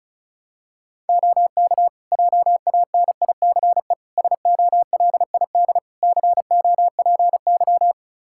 Morse code, heard as a keyed tone, about 34 wpm, 700 Hz. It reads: OK JANICE SOLID COPY
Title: Practice QSO - Exchange 3 - More Info - 35wpm